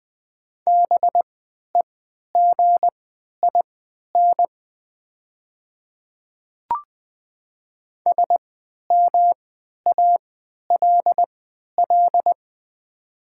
Morse code, heard as a keyed tone, about 20 wpm, spaced as for 11 wpm, 700 Hz. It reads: BEGIN SMALL
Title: 3x Character Spacing - Top 500 Words - 20wpm